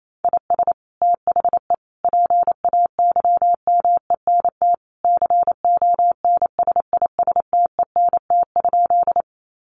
Morse code, heard as a keyed tone, about 28 wpm, 700 Hz. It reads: IS THE PAYMENT CONSISTENT?
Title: Sentences from Top 1500 Words 28wpm